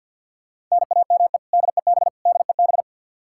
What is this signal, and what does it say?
Morse code, heard as a keyed tone, about 51 wpm, 700 Hz. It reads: NAME DEB DEB
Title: Practice QSO - Exchange 1 - Beginning - 50wpm